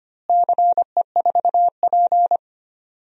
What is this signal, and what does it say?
Morse code, heard as a keyed tone, about 25 wpm, 700 Hz. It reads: CE4P
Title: DXCC-100 Call Signs - 4 Characters - Repeated 2x - 25wpm